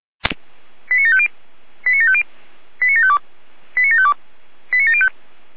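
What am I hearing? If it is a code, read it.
Five-tone selective call (ZVEI), heard as quick runs of five tones, tones of about 70 ms. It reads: 89750 89750 89752 89752 89705
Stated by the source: ZVEI five-tone selective call received on SDR.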